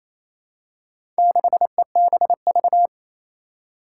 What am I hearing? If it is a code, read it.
Morse code, heard as a keyed tone, about 27 wpm, 700 Hz. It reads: 6EBV
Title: Four Challenging Letter-Number Combinations 28wpm